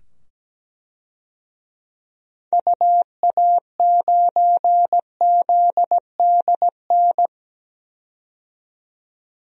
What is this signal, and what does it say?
Morse code, heard as a keyed tone, about 17 wpm, 700 Hz. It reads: UA9ZDN
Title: DXCC-100 Call Signs - 6 Characters - 17wpm